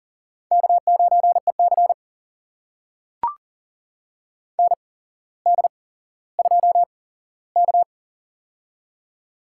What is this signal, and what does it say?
Morse code, heard as a keyed tone, about 40 wpm, spaced as for 17 wpm, 700 Hz. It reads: K9EC ND2K
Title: USA Call Signs - 8x Character Spacing - 40wpm